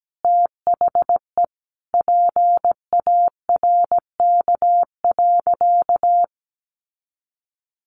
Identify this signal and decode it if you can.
Morse code, heard as a keyed tone, about 17 wpm, 700 Hz. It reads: THE PARK.
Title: Sentences from Top 1400 Words 17wpm